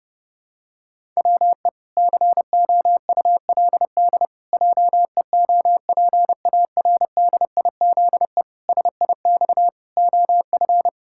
Morse code, heard as a keyed tone, about 32 wpm, 700 Hz. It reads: WE COULD JEOPARDIZE SIX OF
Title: Pangram Challenge v2 - 30wpm